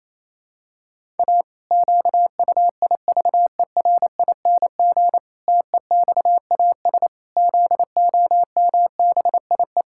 Morse code, heard as a keyed tone, about 27 wpm, 700 Hz. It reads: A QUIVERING TEXAS ZOMBIE
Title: Pangram Challenge v2 - 28wpm